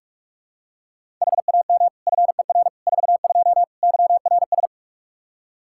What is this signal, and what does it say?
Morse code, heard as a keyed tone, about 42 wpm, 700 Hz. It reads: HAM FER 42 YRS
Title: Practice QSO - Exchange 3 - More Info - 45wpm